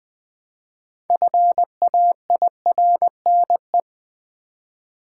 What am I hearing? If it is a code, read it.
Morse code, heard as a keyed tone, about 20 wpm, 700 Hz. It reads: FAIRNE
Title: Build A Word - 2 Letters At A Time - 8-Letter Words - 20wpm